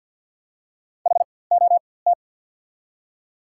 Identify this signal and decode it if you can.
Morse code, heard as a keyed tone, about 54 wpm, spaced as for 24 wpm, 690 Hz. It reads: HOT
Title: 4x Character Spacing - Top 500 Words - 50wpm